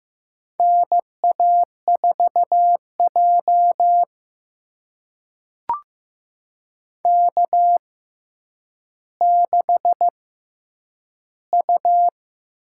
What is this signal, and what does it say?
Morse code, heard as a keyed tone, about 15 wpm, spaced as for 9 wpm, 700 Hz. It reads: NA4J K6U
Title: USA Call Signs - 4 Characters - 6x Character Spacing - 15wpm